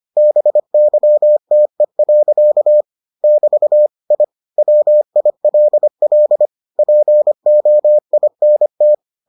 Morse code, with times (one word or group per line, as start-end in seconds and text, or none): 0.2-2.8 BYTE.
3.2-3.9 <BT>
4.1-4.2 I
4.6-6.5 WILL
6.8-8.9 POINT